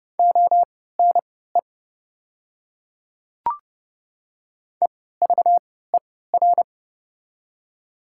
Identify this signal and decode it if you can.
Morse code, heard as a keyed tone, about 29 wpm, spaced as for 17 wpm, 700 Hz. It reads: ONE EVER
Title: 3x Character Spacing - Top 500 Words - 30wpm